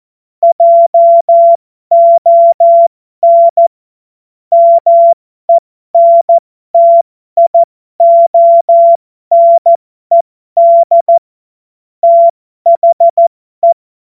Morse code, heard as a keyed tone, about 14 wpm, spaced as for 12 wpm, 680 Hz. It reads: JON MENTIONED THE